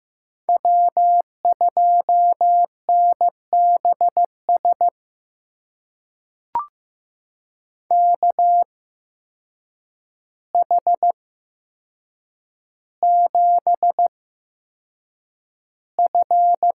Morse code, heard as a keyed tone, about 15 wpm, spaced as for 7 wpm, 700 Hz. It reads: W2NBS KH7F
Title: USA Call Signs - 5 Characters - 8x Character Spacing - 15wpm